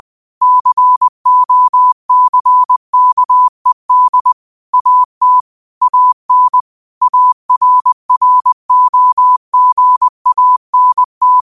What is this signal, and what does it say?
Morse code, heard as a keyed tone, about 20 wpm, 1000 Hz. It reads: COCKED AT AN ARROGANT
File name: qotdfiles / 20wpm